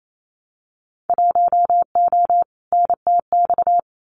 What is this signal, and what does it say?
Morse code, heard as a keyed tone, about 28 wpm, 700 Hz. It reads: 1O NTX
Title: Winter Field Day Exchanges 28wpm